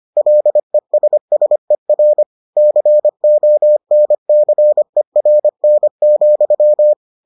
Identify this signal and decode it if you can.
Morse code, heard as a keyed tone, about 25 wpm, 600 Hz. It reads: LESSER CONCERN,